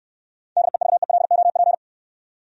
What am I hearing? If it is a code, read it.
Morse code, heard as a keyed tone, about 66 wpm, 700 Hz. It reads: DEVELOP